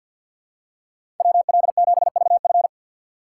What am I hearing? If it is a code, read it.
Morse code, heard as a keyed tone, about 47 wpm, 700 Hz. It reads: WL7VF
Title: USA Call Signs - 5 Characters - 4x Character Spacing - 50wpm